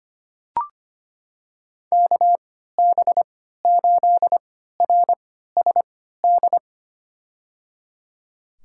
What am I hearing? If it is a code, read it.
Morse code, heard as a keyed tone, about 25 wpm, spaced as for 14 wpm, 700 Hz. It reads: KB8RSD
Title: USA Call Signs - 6 Characters - 3x Character Spacing - 25wpm